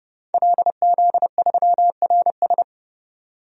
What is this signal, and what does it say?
Morse code, heard as a keyed tone, about 30 wpm, 700 Hz. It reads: LZ3RS
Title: DXCC-100 Call Signs - 5 Characters - 30wpm